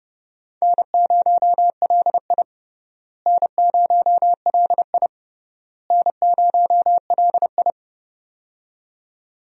Morse code, heard as a keyed tone, about 30 wpm, 700 Hz. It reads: N0LI N0LI N0LI
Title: USA Call Signs - 4 Characters - Repeated 3x - 30wpm